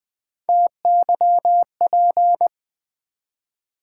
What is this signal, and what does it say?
Morse code, heard as a keyed tone, about 20 wpm, 690 Hz. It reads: TYP
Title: Build A Word - 1 Letter At A Time - 4-Letter Words - 20wpm